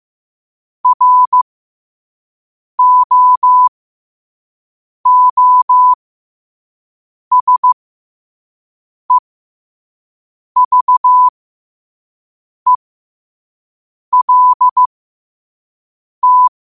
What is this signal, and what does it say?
Morse code, heard as a keyed tone, about 15 wpm, spaced as for 5 wpm, 1000 Hz. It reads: ROOSEVELT